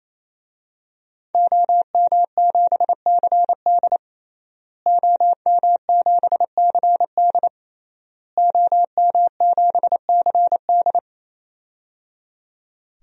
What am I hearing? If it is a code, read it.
Morse code, heard as a keyed tone, about 28 wpm, 700 Hz. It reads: OM7CD OM7CD OM7CD
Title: DXCC-100 Call Signs - 5 Characters - Repeated 3x - 28wpm